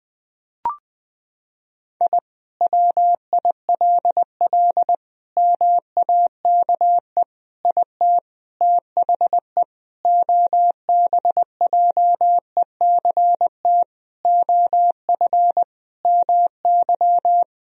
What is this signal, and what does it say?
Morse code, heard as a keyed tone, about 21 wpm, 700 Hz. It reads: I WILL MAKE IT THE OBJECT OF MY
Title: Sentences from Top 1000 Words - Review Part 3 - 20wpm